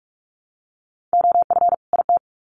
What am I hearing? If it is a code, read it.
Morse code, heard as a keyed tone, about 44 wpm, 700 Hz. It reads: OF IT
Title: Sets of 2 Words 45wpm